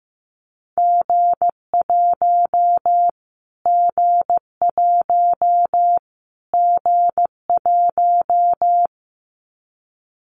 Morse code, heard as a keyed tone, about 15 wpm, 700 Hz. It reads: G1 G1 G1
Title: Callsign Prefixes for Top 25 DXCC - Repeated 3x - 15wpm